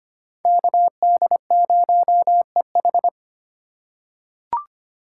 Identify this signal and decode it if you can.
Morse code, heard as a keyed tone, about 25 wpm, 700 Hz. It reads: KD0EH
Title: USA Call Signs - 5 Characters - 2x Character Spacing - 25wpm